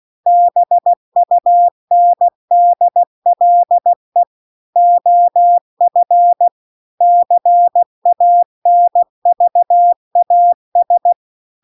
Morse code, heard as a keyed tone, about 16 wpm, 700 Hz. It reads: BUNDLE OF CANVAS
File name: CW-The-Big-Twoheaded-River-1-16WPM